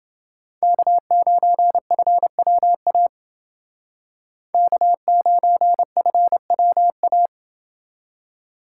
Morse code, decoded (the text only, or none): K9FWA K9FWA